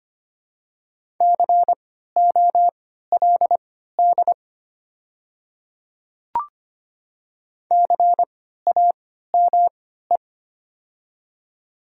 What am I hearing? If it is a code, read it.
Morse code, heard as a keyed tone, about 25 wpm, spaced as for 14 wpm, 700 Hz. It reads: COLD CAME
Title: 3x Character Spacing - Top 500 Words - 25wpm